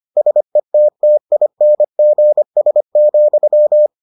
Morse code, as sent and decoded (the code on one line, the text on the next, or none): ... - - .. -. --. ... --..--
SETTINGS,